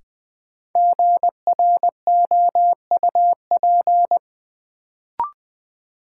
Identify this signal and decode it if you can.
Morse code, heard as a keyed tone, about 20 wpm, 700 Hz. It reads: GROUP
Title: Sentences from Top 1000 Words - Review Part 1 - New Word Drill - 20wpm